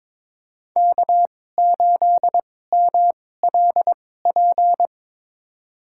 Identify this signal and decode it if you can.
Morse code, heard as a keyed tone, about 22 wpm, spaced as for 16 wpm, 700 Hz. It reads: K8MLP
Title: USA Call Signs - 5 Characters - 2x Character Spacing - 22wpm